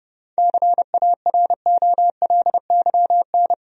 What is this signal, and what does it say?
Morse code, heard as a keyed tone, about 32 wpm, 700 Hz. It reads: CAROLYN
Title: USA Common Names - 30wpm